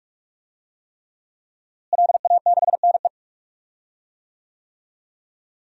Morse code, heard as a keyed tone, about 42 wpm, 700 Hz. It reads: LA6NE
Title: DXCC-100 Call Signs - 5 Characters - Repeated 3x - 45wpm